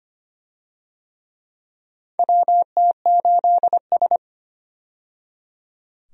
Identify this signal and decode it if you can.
Morse code, heard as a keyed tone, about 25 wpm, 700 Hz. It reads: WT8S
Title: USA Call Signs - 4 Characters - 25wpm